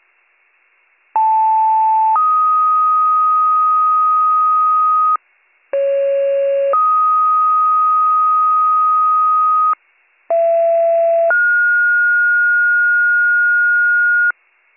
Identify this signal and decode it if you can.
Two-tone page, heard as a one-second tone, then a three-second tone, about 880 then 1250 Hz, about 550 then 1250 Hz, about 670 then 1430 Hz. It is heard from 1.2 to 14.3 s.